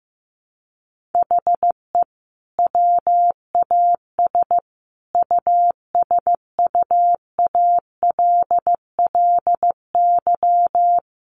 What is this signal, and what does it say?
Morse code, heard as a keyed tone, about 15 wpm, 700 Hz. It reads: HE WAS USUALLY
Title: Sentences from Top 1400 Words 15wpm